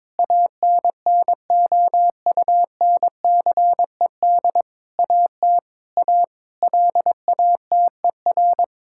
Morse code, heard as a keyed tone, about 22 wpm, 700 Hz. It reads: ANNOUNCED AT A LATER
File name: Sentences with 7-Word Phrases - 22wpm